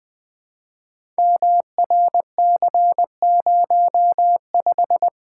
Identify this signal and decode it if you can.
Morse code, heard as a keyed tone, about 20 wpm, 700 Hz. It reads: MRC05
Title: Code Groups - 5 Letter:Numbers - 20wpm